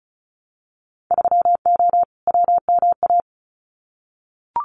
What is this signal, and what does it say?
Morse code, heard as a keyed tone, about 36 wpm, 700 Hz. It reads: 3O WMA
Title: Winter Field Day Exchanges Encore 35wpm